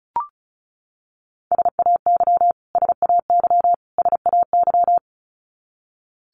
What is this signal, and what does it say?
Morse code, heard as a keyed tone, about 34 wpm, 700 Hz. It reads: SAY SAY SAY
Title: Top 500 Words Repeated 35wpm